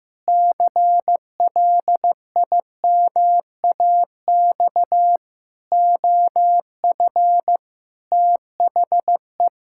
Morse code, heard as a keyed tone, about 15 wpm, 700 Hz. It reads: CLIMAX OF THE
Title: Pangram Challenge v2 - 15wpm